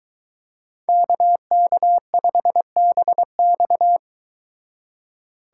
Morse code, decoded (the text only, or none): KK5BX